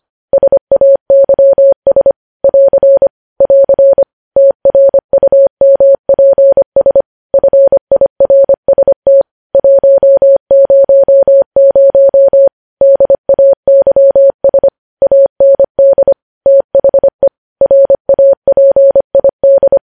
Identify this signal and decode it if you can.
Morse code, heard as a keyed tone, about 25 wpm, 550 Hz. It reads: SAYS <AR> <AR> TRUMPS FIRST 100 DAYS AND THE RAPID
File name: podcast / News headlines-1x-25x25